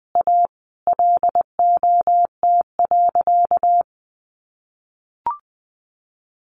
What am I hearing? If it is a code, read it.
Morse code, heard as a keyed tone, about 20 wpm, 700 Hz. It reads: A LOT.
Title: Easy Phrases 20wpm